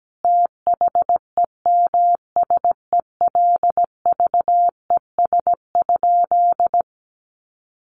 Morse code, heard as a keyed tone, about 17 wpm, 700 Hz. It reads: THEMSELVES?
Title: Sentences from Top 800 Words 17wpm